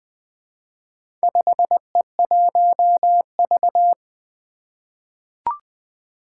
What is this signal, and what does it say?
Morse code, heard as a keyed tone, about 21 wpm, 700 Hz. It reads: 5E1V
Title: Four Challenging Letter-Number Combinations 20wpm